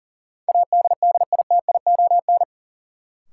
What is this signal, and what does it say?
Morse code, heard as a keyed tone, about 37 wpm, 700 Hz. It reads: ADDITION
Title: Sentences from Top 1500 Words - New Word Drill Alt - 40wpm